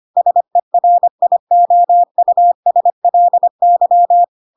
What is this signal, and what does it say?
Morse code, heard as a keyed tone, about 25 wpm, 700 Hz. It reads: SERIOUSLY
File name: Pinocchio / Pinocchio ch3 25 25 ik1hge